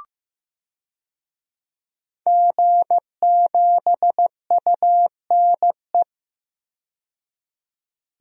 Morse code, heard as a keyed tone, about 15 wpm, 700 Hz. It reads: G7UNE
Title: DXCC-100 Call Signs - 5 Characters - 15wpm